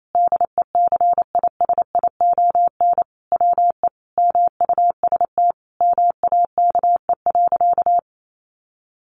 Morse code, heard as a keyed tone, about 28 wpm, 700 Hz. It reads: DECISION WE MUST MAKE.
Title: Sentences from Top 1000 Words 28wpm